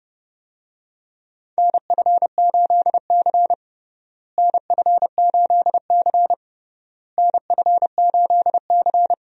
Morse code, heard as a keyed tone, about 30 wpm, 700 Hz. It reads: NF8C NF8C NF8C
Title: USA Call Signs - 4 Characters - Repeated 3x - 30wpm